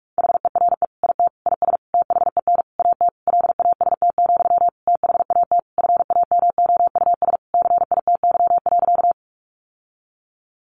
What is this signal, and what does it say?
Morse code, heard as a keyed tone, about 44 wpm, 700 Hz. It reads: HERE IT IS THEN AT LAST, THAT FAMOUS CITY.
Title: Sentences from Top 1500 Words 45wpm